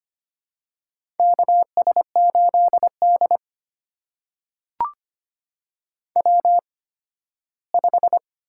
Morse code, heard as a keyed tone, about 25 wpm, spaced as for 15 wpm, 700 Hz. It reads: KS8D W5
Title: USA Call Signs - 4 Characters - 8x Character Spacing - 25wpm